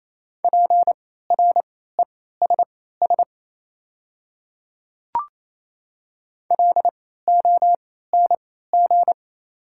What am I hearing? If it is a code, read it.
Morse code, heard as a keyed tone, about 28 wpm, spaced as for 16 wpm, 700 Hz. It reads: PRESS LONG
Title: 3x Character Spacing - Top 300 Words - 28wpm